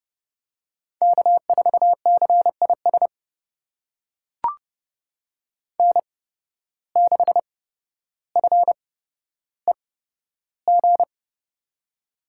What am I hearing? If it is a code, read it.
Morse code, heard as a keyed tone, about 30 wpm, spaced as for 13 wpm, 700 Hz. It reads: K4CIS N6FEG
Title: USA Call Signs - 5 Characters - 8x Character Spacing - 30wpm